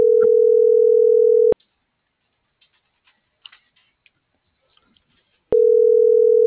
Ringback tone (North America), 440 + 480 Hz.
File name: CENTRAL SKYBLUE / out lopez.brandon